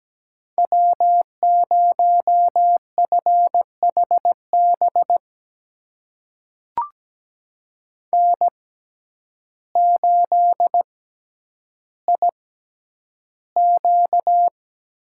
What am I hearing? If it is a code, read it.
Morse code, heard as a keyed tone, about 17 wpm, spaced as for 9 wpm, 700 Hz. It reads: W0FHB N8IQ
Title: USA Call Signs - 5 Characters - 6x Character Spacing - 17wpm